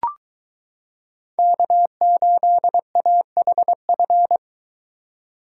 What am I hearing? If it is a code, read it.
Morse code, heard as a keyed tone, about 23 wpm, 700 Hz. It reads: K8AHF